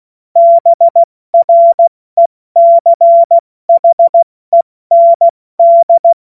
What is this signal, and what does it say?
Morse code, heard as a keyed tone, about 16 wpm, 680 Hz. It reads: BRECHEND
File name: Morsekurs / Witze / Witz 12 März 2024 70 F80